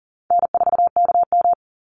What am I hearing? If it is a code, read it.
Morse code, heard as a keyed tone, about 40 wpm, 700 Hz. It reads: N4KM
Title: Easy Call Signs - 40wpm